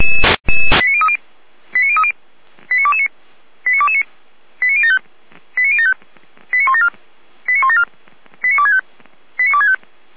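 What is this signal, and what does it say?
Five-tone selective call (ZVEI), heard as quick runs of five tones, tones of about 70 ms. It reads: 89030 89030 89229 89229 89075 89075 89174 89174 89265 89265
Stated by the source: ZVEI five-tone call received by SDR, from the Feuerwehr Orlamünde audio server.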